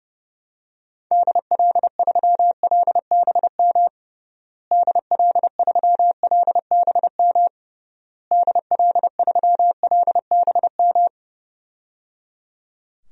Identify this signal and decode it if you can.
Morse code, heard as a keyed tone, about 30 wpm, 700 Hz. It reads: DL3LBM DL3LBM DL3LBM
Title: DXCC-100 Call Signs - 6 Characters - Repeated 3x - 30wpm